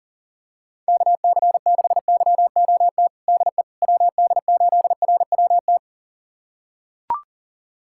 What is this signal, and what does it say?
Morse code, heard as a keyed tone, about 40 wpm, 700 Hz. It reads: KC6YOT DE WD8RWT
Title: USA Callsign DE Callsign 40wpm